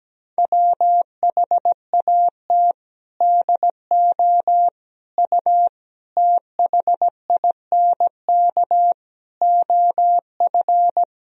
Morse code, heard as a keyed tone, about 17 wpm, 700 Hz. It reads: WHAT DO U THINK OF
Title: Sentences from Top 200 Words and Abbreviations 17wpm